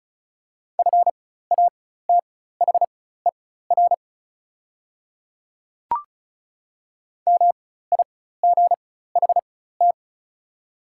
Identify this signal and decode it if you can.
Morse code, heard as a keyed tone, about 34 wpm, spaced as for 16 wpm, 700 Hz. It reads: FATHER MIGHT
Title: 4x Character Spacing - Top 300 Words - 35wpm